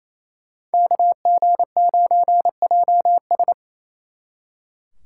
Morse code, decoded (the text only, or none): KG9JS